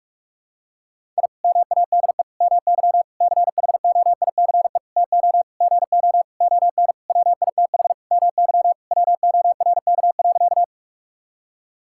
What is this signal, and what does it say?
Morse code, heard as a keyed tone, about 41 wpm, 700 Hz. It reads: I MADE MY CHOICE TO GO ON WITH MY WORK.
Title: Sentences from Top 1000 Words - Review Part 6 - 45wpm